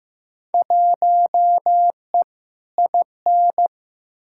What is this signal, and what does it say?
Morse code, heard as a keyed tone, about 15 wpm, 700 Hz. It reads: 1E IN
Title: Field Day Exchanges Encore - 15wpm